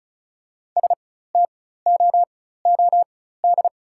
Morse code, heard as a keyed tone, about 36 wpm, spaced as for 16 wpm, 690 Hz. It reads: STOOD
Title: 4x Character Spacing - Top 500 Words - 35wpm